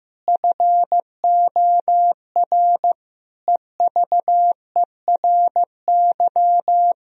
Morse code, heard as a keyed tone, about 15 wpm, 700 Hz. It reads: FOR EVERY
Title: Sentences from Top 1000 Words - Review Part 1 - 15wpm